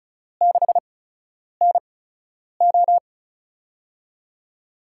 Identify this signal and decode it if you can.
Morse code, heard as a keyed tone, about 36 wpm, spaced as for 10 wpm, 700 Hz. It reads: 6NO
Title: USA Call Signs - 4 Characters - 8x Character Spacing - 35wpm